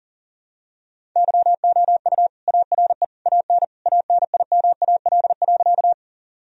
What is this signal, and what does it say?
Morse code, heard as a keyed tone, about 40 wpm, 700 Hz. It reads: YOU ARE AN ANIMAL.
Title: Sentences from Top 1000 Words - Review Part 1 - 40wpm